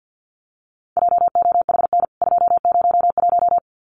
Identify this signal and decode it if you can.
Morse code, heard as a keyed tone, about 50 wpm, 700 Hz. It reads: JOHN 201
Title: CWT Exchange 50wpm